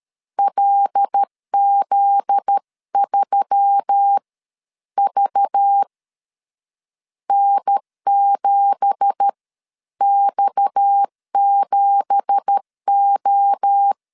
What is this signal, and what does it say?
Morse code, heard as a keyed tone, about 13 wpm, spaced as for 9 wpm, 800 Hz. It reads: LZ3VN 7X7O
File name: CW / DET / 1 9 1500C S B 12wpm 3819